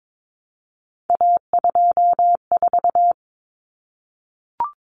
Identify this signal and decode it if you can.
Morse code, heard as a keyed tone, about 22 wpm, 700 Hz. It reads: A24
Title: Common Cut Numbers - 3-Digit 22wpm